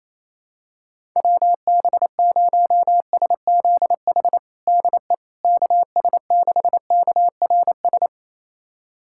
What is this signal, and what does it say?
Morse code, heard as a keyed tone, about 27 wpm, 700 Hz. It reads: WB0SZH DE KS6KRS
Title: USA Callsign DE Callsign 28wpm